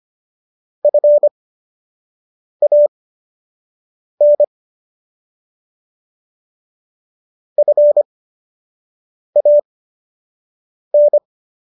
Morse code, heard as a keyed tone, about 25 wpm, spaced as for 6 wpm, 590 Hz. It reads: FAN FAN
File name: qso002 06